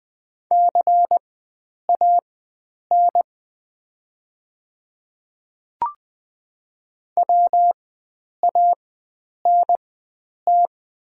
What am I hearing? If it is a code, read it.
Morse code, heard as a keyed tone, about 20 wpm, spaced as for 9 wpm, 700 Hz. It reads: CAN WANT
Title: 4x Character Spacing - Top 300 Words - 20wpm